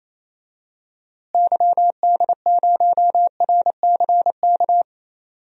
Morse code, heard as a keyed tone, about 27 wpm, 700 Hz. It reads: YD0RCK
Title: DXCC-100 Call Signs - 6 Characters - Repeated 3x - 28wpm